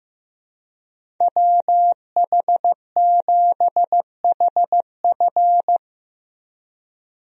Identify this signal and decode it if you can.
Morse code, heard as a keyed tone, about 15 wpm, 700 Hz. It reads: WH7HF
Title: USA Call Signs - 5 Characters - Repeated 3x - 15wpm